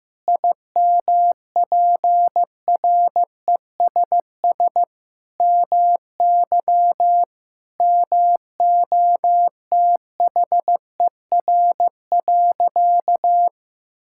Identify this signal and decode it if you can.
Morse code, heard as a keyed tone, about 16 wpm, 700 Hz. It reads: IMPRESS MY MOTHER.
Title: Sentences from Top 1600 Words 15wpm